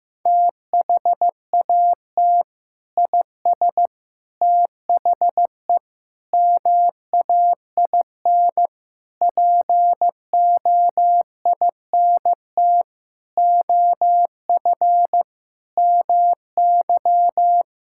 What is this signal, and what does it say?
Morse code, heard as a keyed tone, about 16 wpm, 700 Hz. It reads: THAT IS THE MAIN POINT OF MY
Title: Sentences from Top 1000 Words - Review Part 5 - 15wpm